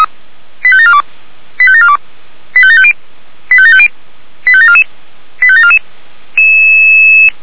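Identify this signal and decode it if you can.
Five-tone selective call (ZVEI), heard as quick runs of five tones, tones of about 70 ms. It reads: 85742 85742 85750 85750 85744 85744
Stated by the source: ZVEI five-tone call received by SDR.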